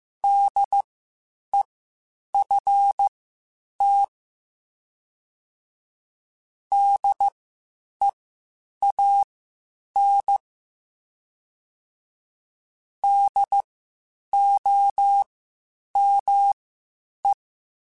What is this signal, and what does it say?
Morse code, heard as a keyed tone, about 15 wpm, spaced as for 8 wpm, 800 Hz. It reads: DEFT DEAN DOME
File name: Spaced Alphas / Spaced Alphas mp3s / Ds sp @13